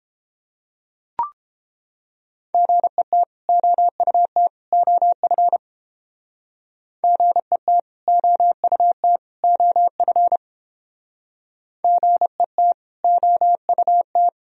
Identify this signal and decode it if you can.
Morse code, heard as a keyed tone, about 30 wpm, 700 Hz. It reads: GET OUT OF GET OUT OF GET OUT